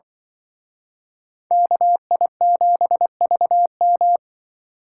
Morse code, heard as a keyed tone, about 23 wpm, 700 Hz. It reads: KI7VM
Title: Speed Racing - USA Call Signs 20wpm